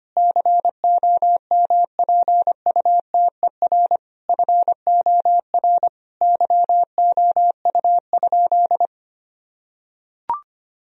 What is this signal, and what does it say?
Morse code, heard as a keyed tone, about 25 wpm, 700 Hz. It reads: COMPUTER FOR YOU?
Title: Sentences from Top 1000 Words - Review Part 3 - 25wpm